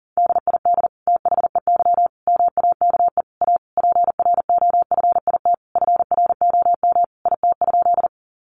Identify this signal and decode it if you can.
Morse code, heard as a keyed tone, about 40 wpm, 700 Hz. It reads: DID THEY MAKE A PROFIT FROM IT?
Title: Sentences from Top 800 Words 40wpm